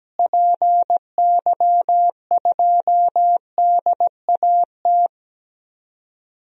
Morse code, heard as a keyed tone, about 17 wpm, 700 Hz. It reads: PY2DAT
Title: DXCC-100 Call Signs - 6 Characters - Repeated 3x - 17wpm